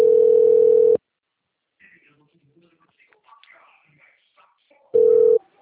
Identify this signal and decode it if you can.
Ringback tone (North America), 440 + 480 Hz.